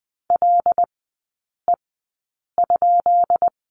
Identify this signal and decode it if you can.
Morse code, heard as a keyed tone, about 20 wpm, spaced as for 8 wpm, 700 Hz. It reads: LE?
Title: 138 Lesson 23e - Letter f - Three Characters 20wpm